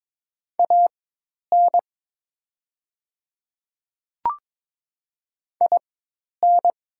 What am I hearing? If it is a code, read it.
Morse code, heard as a keyed tone, about 22 wpm, spaced as for 10 wpm, 700 Hz. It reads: AN IN
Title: 4x Character Spacing - Top 200 Words - 22wpm